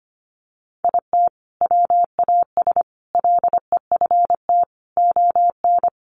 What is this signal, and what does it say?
Morse code, heard as a keyed tone, about 25 wpm, 700 Hz. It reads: IT WAS LEFT ON
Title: Sentences from Top 700 Words 25wpm